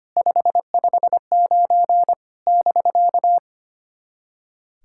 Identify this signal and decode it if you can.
Morse code, heard as a keyed tone, about 25 wpm, 700 Hz. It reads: 559 <BK>
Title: SOTA QSO Practice 25wpm